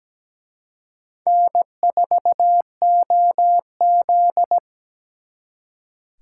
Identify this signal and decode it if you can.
Morse code, heard as a keyed tone, about 20 wpm, 700 Hz. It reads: N4OZ
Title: Speed Racing - USA 1x2 Call Signs 17wpm